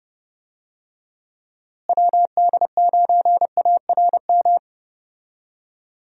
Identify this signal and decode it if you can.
Morse code, heard as a keyed tone, about 32 wpm, 700 Hz. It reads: WD9ARM
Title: USA Call Signs - 6 Characters - Repeated 2x - 30wpm